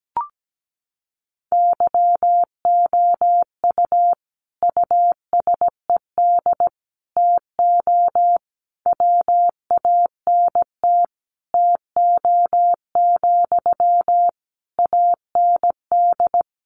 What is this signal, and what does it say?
Morse code, heard as a keyed tone, about 17 wpm, 700 Hz. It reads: YOU USED TO WANT TO, AND
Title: Sentences from Top 600 Words 17wpm